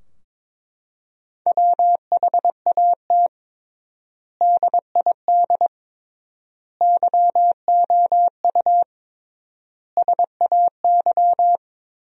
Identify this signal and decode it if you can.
Morse code, heard as a keyed tone, about 22 wpm, 700 Hz. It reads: WHAT DID YOU SAY
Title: 3x Word-Spacing - Sentences from Top 200 Words - 22wpm